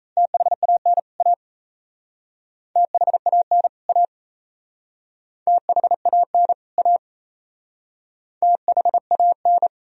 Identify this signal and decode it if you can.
Morse code, heard as a keyed tone, about 37 wpm, 700 Hz. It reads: THAN A THAN A THAN A THAN